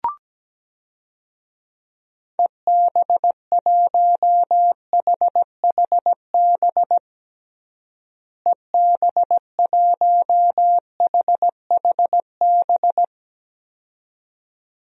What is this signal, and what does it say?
Morse code, heard as a keyed tone, about 17 wpm, 700 Hz. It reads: EB1HHB EB1HHB
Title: DXCC-100 Call Signs - 6 Characters - Repeated 2x - 17wpm